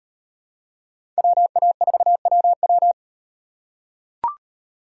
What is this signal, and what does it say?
Morse code, heard as a keyed tone, about 38 wpm, 700 Hz. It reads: WA4WW